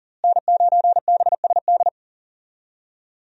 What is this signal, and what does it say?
Morse code, heard as a keyed tone, about 40 wpm, 700 Hz. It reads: N9BSD
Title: USA Call Signs - 5 Characters - Repeated 2x - 40wpm